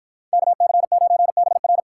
Morse code, heard as a keyed tone, about 58 wpm, 700 Hz. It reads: KC9BR